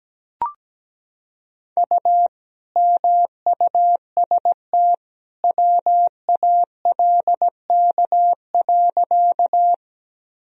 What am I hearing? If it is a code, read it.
Morse code, heard as a keyed tone, about 17 wpm, 700 Hz. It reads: U MUST WALK.
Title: Sentences from Top 300 Words and Abbreviations 17wpm